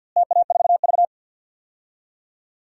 Morse code, heard as a keyed tone, about 49 wpm, 700 Hz. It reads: TA4V
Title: DXCC-100 Call Signs - 4 Characters - Repeated 3x - 50wpm